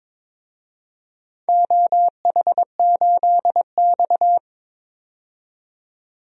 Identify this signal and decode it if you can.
Morse code, heard as a keyed tone, about 22 wpm, 700 Hz. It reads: OH8X
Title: DXCC-100 Call Signs - 4 Characters - 22wpm